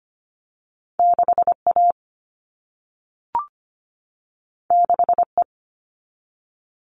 Morse code, heard as a keyed tone, about 25 wpm, 700 Hz. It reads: 6A 6E
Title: All Cut Numbers - 2-Digit 25wpm